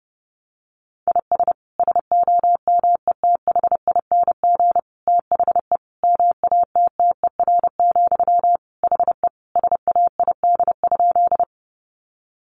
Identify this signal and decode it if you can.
Morse code, heard as a keyed tone, about 30 wpm, 700 Hz. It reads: IS SOMETHING THE MATTER, HE SAID?
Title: Sentences from Top 700 Words 30wpm